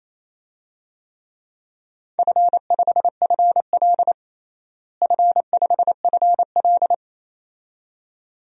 Morse code, heard as a keyed tone, about 28 wpm, 700 Hz. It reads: F5FL F5FL
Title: DXCC-100 Call Signs - 4 Characters - Repeated 2x - 28wpm